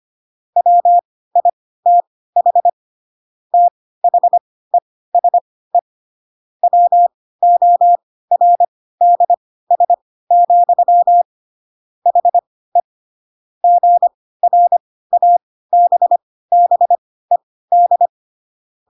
Morse code, heard as a keyed tone, about 25 wpm, spaced as for 16 wpm, 700 Hz. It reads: WITH THESE WORDS, HE GRABBED
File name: Pinocchio / Pinocchio ch1 10 25 ik1hge